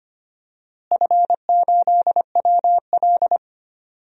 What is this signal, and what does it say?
Morse code, heard as a keyed tone, about 25 wpm, 700 Hz. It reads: F8WL
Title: DXCC-100 Call Signs - 4 Characters - Repeated 2x - 25wpm